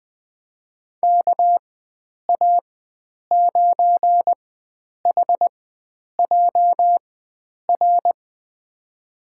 Morse code, heard as a keyed tone, about 20 wpm, spaced as for 9 wpm, 700 Hz. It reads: KA9HJR
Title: USA Call Signs - 6 Characters - 4x Character Spacing - 20wpm